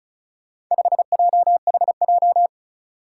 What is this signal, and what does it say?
Morse code, heard as a keyed tone, about 36 wpm, 700 Hz. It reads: 5JHJ
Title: Four Challenging Letter-Number Combinations 35wpm